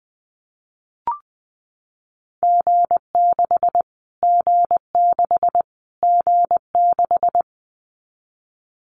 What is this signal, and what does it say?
Morse code, heard as a keyed tone, about 19 wpm, 700 Hz. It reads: G6 G6 G6
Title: Callsign Prefixes for Top 25 DXCC - Repeated 3x - 20wpm